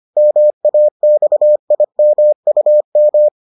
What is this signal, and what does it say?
Morse code, heard as a keyed tone, about 25 wpm, 600 Hz. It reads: MAXIMUM